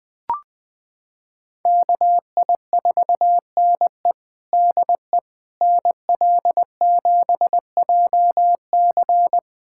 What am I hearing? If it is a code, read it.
Morse code, heard as a keyed tone, about 20 wpm, 700 Hz. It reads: KI4NE DE NL7JC
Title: USA Callsign DE Callsign 20wpm